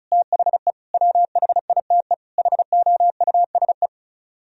Morse code, heard as a keyed tone, about 44 wpm, 700 Hz. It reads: THE WHITE HOUSE